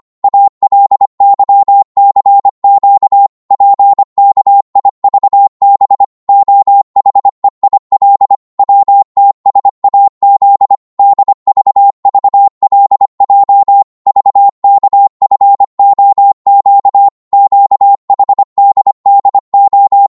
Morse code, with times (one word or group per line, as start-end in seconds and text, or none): 0.2-3.3 ALYCQ
3.5-6.0 PKIVB
6.3-8.3 OHEIL
8.6-10.8 WTSAZ
11.0-13.8 DVVLJ
14.1-17.1 VKFOQ
17.3-20.1 QHDDO